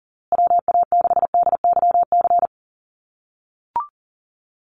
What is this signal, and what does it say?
Morse code, heard as a keyed tone, about 40 wpm, 700 Hz. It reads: WA6DYC
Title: USA Call Signs Repeated 3x 40wpm